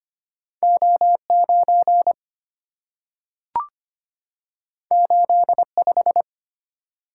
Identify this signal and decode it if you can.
Morse code, heard as a keyed tone, about 25 wpm, 700 Hz. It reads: O9 85
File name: Two Letter-Number Combinations - 25wpm